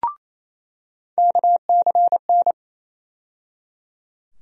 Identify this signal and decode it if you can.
Morse code, heard as a keyed tone, about 28 wpm, 700 Hz. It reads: KCN
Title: USA Call Sign Postfixes - 3 Letters 28wpm